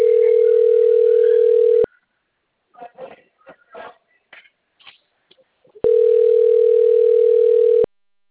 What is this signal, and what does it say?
Ringback tone (North America), 440 + 480 Hz.
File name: rg-1600-unknown- - - .92730